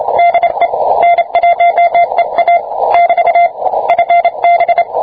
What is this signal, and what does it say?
Morse code, heard as a keyed tone, about 27 wpm, 690 Hz. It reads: DE N1EA <BT> FB